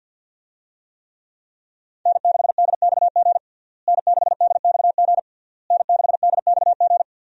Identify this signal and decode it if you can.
Morse code, heard as a keyed tone, about 50 wpm, 700 Hz. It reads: N6DXG N6DXG N6DXG
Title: USA Call Signs - 5 Characters - Repeated 3x - 50wpm